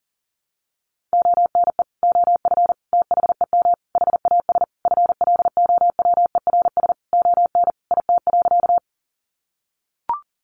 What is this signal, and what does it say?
Morse code, heard as a keyed tone, about 40 wpm, 700 Hz. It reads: ONE OF THEM HAS FLOWERS ON IT.
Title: Additional Easy Phrases 40wpm